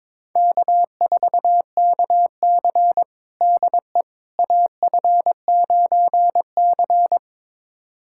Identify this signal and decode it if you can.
Morse code, heard as a keyed tone, about 22 wpm, 700 Hz. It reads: K4KC DE AF9C
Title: USA Callsign DE Callsign - 4 Characters - 22wpm